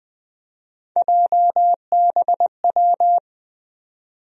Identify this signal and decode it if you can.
Morse code, heard as a keyed tone, about 20 wpm, 700 Hz. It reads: JBW
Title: USA Call Sign Postfixes - 3 Letters 20wpm